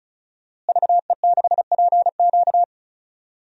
Morse code, heard as a keyed tone, about 36 wpm, 700 Hz. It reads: VE6PQ
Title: DXCC-100 Call Signs - 5 Characters - Repeated 3x - 35wpm